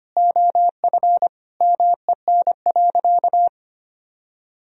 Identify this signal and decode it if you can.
Morse code, heard as a keyed tone, about 25 wpm, 700 Hz. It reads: OF MEN.
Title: Sentences from Top 1000 Words - Review Part 1 - 25wpm